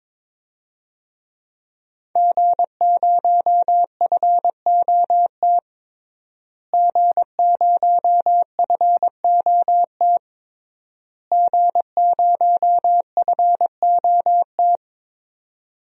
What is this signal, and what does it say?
Morse code, heard as a keyed tone, about 22 wpm, 700 Hz. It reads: G0FOT G0FOT G0FOT
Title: DXCC-100 Call Signs - 5 Characters - Repeated 3x - 22wpm